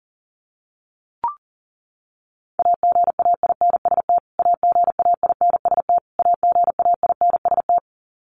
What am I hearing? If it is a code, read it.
Morse code, heard as a keyed tone, about 40 wpm, 700 Hz. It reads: AGAINST AGAINST AGAINST
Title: Top 500 Words Repeated 40wpm